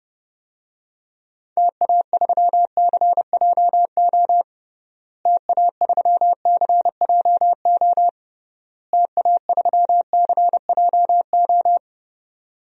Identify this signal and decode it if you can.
Morse code, heard as a keyed tone, about 32 wpm, 700 Hz. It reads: TA3CJO TA3CJO TA3CJO
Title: DXCC-100 Call Signs - 6 Characters - Repeated 3x - 30wpm